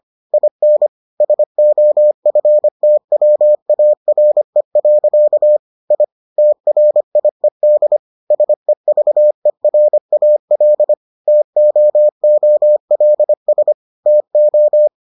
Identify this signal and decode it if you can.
Morse code, heard as a keyed tone, about 25 wpm, 600 Hz. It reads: IN SOFTWARE. I TRIED SEVERAL TOOLS TO